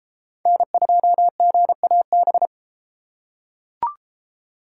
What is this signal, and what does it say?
Morse code, heard as a keyed tone, about 29 wpm, 700 Hz. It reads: N2GAB